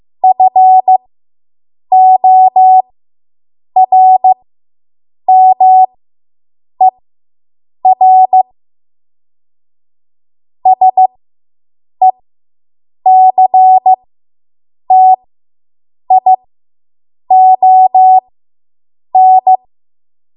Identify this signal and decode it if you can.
Morse code, heard as a keyed tone, about 15 wpm, spaced as for 7 wpm, 750 Hz. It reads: FORMER SECTION